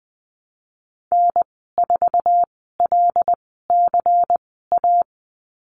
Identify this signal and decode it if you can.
Morse code, heard as a keyed tone, about 20 wpm, spaced as for 15 wpm, 700 Hz. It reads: N4LCA
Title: 122 Lesson 20g - Number 3 - Call Signs 20wpm